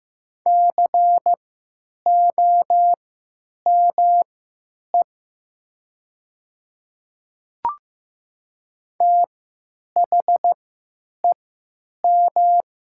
Morse code, heard as a keyed tone, about 15 wpm, spaced as for 9 wpm, 690 Hz. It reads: COME THEM
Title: 3x Character Spacing - Top 100 Words - 15wpm